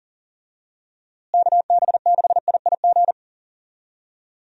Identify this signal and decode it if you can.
Morse code, heard as a keyed tone, about 40 wpm, 700 Hz. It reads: KB6IIG
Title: USA Call Signs - 6 Characters - Repeated 2x - 40wpm